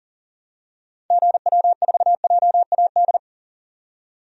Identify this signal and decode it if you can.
Morse code, heard as a keyed tone, about 37 wpm, 700 Hz. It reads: GW4JAD
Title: DXCC-100 Call Signs - 6 Characters - 40wpm